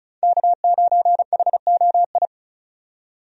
Morse code, heard as a keyed tone, about 36 wpm, 700 Hz. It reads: K9HOI
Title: USA Call Signs - 5 Characters - 35wpm